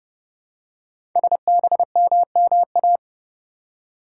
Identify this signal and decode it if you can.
Morse code, heard as a keyed tone, about 30 wpm, 700 Hz. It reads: SBMMA
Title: Code Groups - 5 Letters - 30wpm